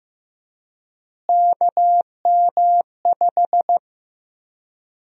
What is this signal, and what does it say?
Morse code, heard as a keyed tone, about 15 wpm, 700 Hz. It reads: KM5
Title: USA Call Sign Prefixes - Repeated 2x - 15wpm